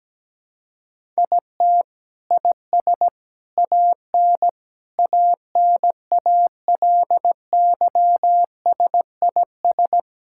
Morse code, heard as a keyed tone, about 17 wpm, 700 Hz. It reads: IT IS AN ANALYSIS
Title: Sentences from Top 1000 Words - Review Part 5 - 17wpm